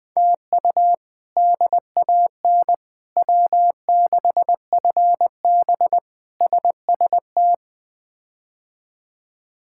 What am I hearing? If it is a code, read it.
Morse code, heard as a keyed tone, about 21 wpm, 700 Hz. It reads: TU DAN W6FB SST
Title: K1USN SST QSO Practice - 20wpm